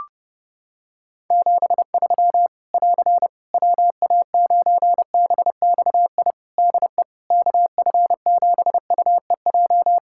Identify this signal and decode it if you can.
Morse code, heard as a keyed tone, about 30 wpm, 700 Hz. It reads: 73 <AR> WA9BXI DE KF7UEJ
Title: K1USN SST QSO Practice 30wpm